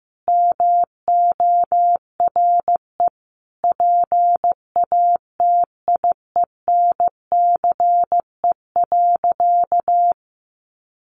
Morse code, heard as a keyed tone, about 15 wpm, 700 Hz. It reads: MORE PATIENCE.
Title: Sentences from Top 1300 Words 15wpm